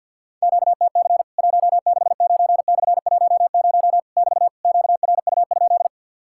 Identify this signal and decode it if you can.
Morse code, heard as a keyed tone, about 50 wpm, 700 Hz. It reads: QTC 169/10 <BT> QRU?
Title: Worked All Europe DX Contest - QTC Exchange - 50wpm